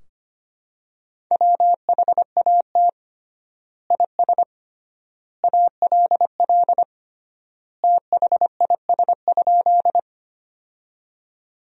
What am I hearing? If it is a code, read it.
Morse code, heard as a keyed tone, about 25 wpm, 700 Hz. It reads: WHAT IS ALL THIS?
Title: 3x Word-Spacing - Sentences from Top 100 Words - 25wpm